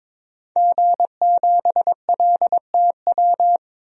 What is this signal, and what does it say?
Morse code, heard as a keyed tone, about 22 wpm, 700 Hz. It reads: G7LTW
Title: DXCC-100 Call Signs - 5 Characters - 22wpm